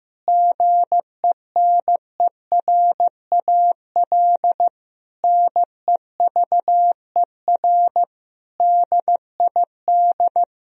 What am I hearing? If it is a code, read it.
Morse code, heard as a keyed tone, about 15 wpm, 700 Hz. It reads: GENERAL NEVER DID
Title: Sentences from Top 1000 Words - Review Part 4 - 15wpm